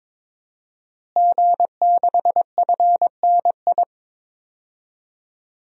Morse code, heard as a keyed tone, about 22 wpm, 700 Hz. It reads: G6FNI
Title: DXCC-100 Call Signs - 5 Characters - Repeated 3x - 22wpm